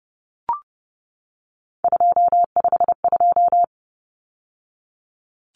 Morse code, heard as a keyed tone, about 29 wpm, 700 Hz. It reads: 252
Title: Numbers - 3-Digit 30wpm